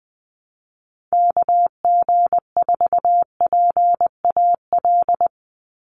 Morse code, heard as a keyed tone, about 20 wpm, 700 Hz. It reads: KG4PAL
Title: USA Call Signs 20wpm